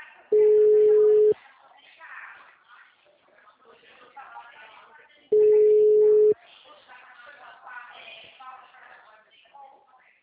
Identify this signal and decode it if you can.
Ringback tone (Europe, 425 Hz), with speech.